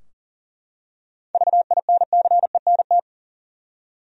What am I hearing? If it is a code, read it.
Morse code, heard as a keyed tone, about 40 wpm, 700 Hz. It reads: VINCENT
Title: USA Common Names - 40wpm